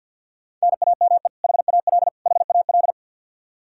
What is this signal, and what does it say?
Morse code, heard as a keyed tone, about 51 wpm, 700 Hz. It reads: NAME HAL HAL
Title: Practice QSO - Exchange 1 - Beginning - 50wpm